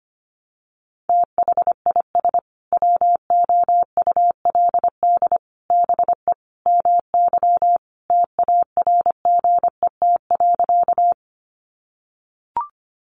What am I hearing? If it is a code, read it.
Morse code, heard as a keyed tone, about 25 wpm, 700 Hz. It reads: THIS WOULD BE MY TARGET.
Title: Sentences from Top 1200 Words 25wpm